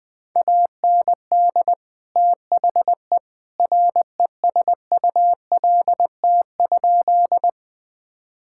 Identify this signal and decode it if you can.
Morse code, heard as a keyed tone, about 20 wpm, 700 Hz. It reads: AND THE RESULT?
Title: Sentences from Top 1000 Words - Review Part 5 - 20wpm